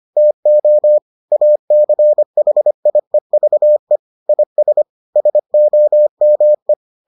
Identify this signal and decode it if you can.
Morse code, heard as a keyed tone, about 25 wpm, 600 Hz. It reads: TO ACHIEVE IS SOME